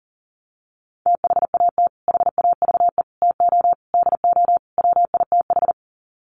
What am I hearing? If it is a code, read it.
Morse code, heard as a keyed tone, about 40 wpm, 700 Hz. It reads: THAT HAVE TO DO WITH
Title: Sets of 5 Words Encore2 40wpm